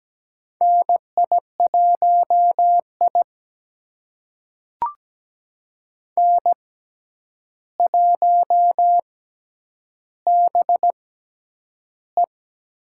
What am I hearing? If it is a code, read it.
Morse code, heard as a keyed tone, about 17 wpm, spaced as for 9 wpm, 700 Hz. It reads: NI1I N1BE
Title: USA Call Signs - 4 Characters - 6x Character Spacing - 17wpm